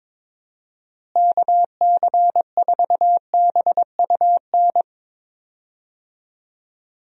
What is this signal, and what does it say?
Morse code, heard as a keyed tone, about 22 wpm, 700 Hz. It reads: KC4BUN
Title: USA Call Signs - 6 Characters - Repeated 2x - 22wpm